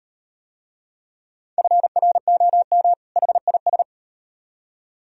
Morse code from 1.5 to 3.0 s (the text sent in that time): FROM